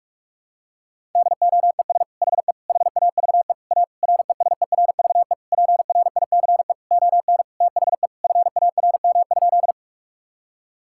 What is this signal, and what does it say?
Morse code, heard as a keyed tone, about 46 wpm, 700 Hz. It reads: DOES HE HAVE A RESERVE PRICE ON THE FARM?
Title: Sentences from Top 1000 Words - Review Part 6 - 45wpm